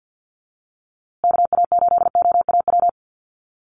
Morse code, heard as a keyed tone, about 50 wpm, 700 Hz. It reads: KA8OAW
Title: USA Call Signs Repeated 3x 50wpm